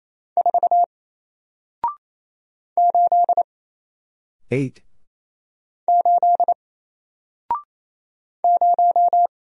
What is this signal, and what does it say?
Morse code, heard as a keyed tone, about 27 wpm, 700 Hz. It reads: 4 8 8 0
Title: Single Character (Letter:Number:Punctuation:Prosign) 28wpm